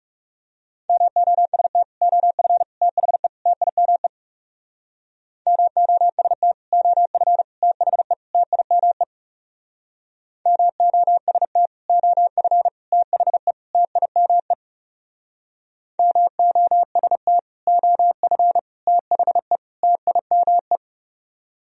Morse code, heard as a keyed tone, about 37 wpm, 700 Hz. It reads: MOST OF THE TIME MOST OF THE TIME MOST OF THE TIME MOST OF THE TIME
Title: Speed-Racing - Sets of 4 Words 30wpm